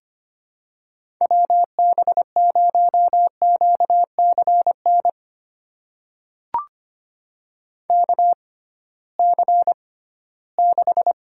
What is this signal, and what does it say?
Morse code, heard as a keyed tone, about 25 wpm, spaced as for 16 wpm, 700 Hz. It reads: WB0QCN KC6
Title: USA Call Signs - 6 Characters - 6x Character Spacing - 25wpm